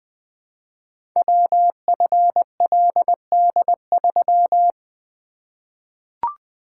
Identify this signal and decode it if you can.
Morse code, heard as a keyed tone, about 20 wpm, 700 Hz. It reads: WFLD3
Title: Code Groups - 5 Letter:Numbers - 20wpm